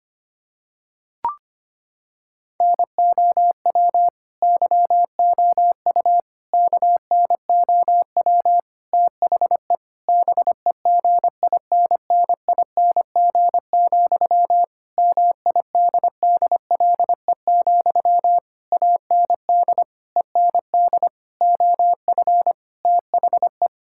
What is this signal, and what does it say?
Morse code, heard as a keyed tone, about 25 wpm, 700 Hz. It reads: NOW YOU KNOW THE BEGINNING, MIDDLE, AND END OF THE
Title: Sentences from Top 1000 Words - Review Part 6 - 25wpm